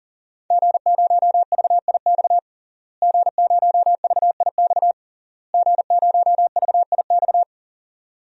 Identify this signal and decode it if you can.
Morse code, heard as a keyed tone, about 40 wpm, 700 Hz. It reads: G0VIX G0VIX G0VIX
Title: DXCC-100 Call Signs - 5 Characters - Repeated 3x - 40wpm